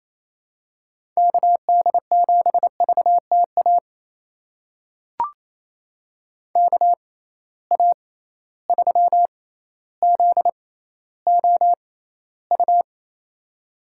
Morse code, heard as a keyed tone, about 28 wpm, spaced as for 14 wpm, 700 Hz. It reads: KD7VTA KA3ZOU
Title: USA Call Signs - 6 Characters - 6x Character Spacing - 28wpm